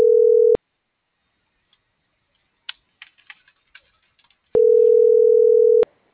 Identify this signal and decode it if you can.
Ringback tone (North America), 440 + 480 Hz.